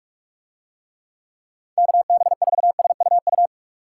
Morse code, heard as a keyed tone, about 46 wpm, 700 Hz. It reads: KB4SUU